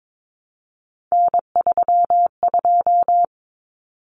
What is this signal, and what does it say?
Morse code, heard as a keyed tone, about 22 wpm, 700 Hz. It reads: N32
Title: Common Cut Numbers - 3-Digit 22wpm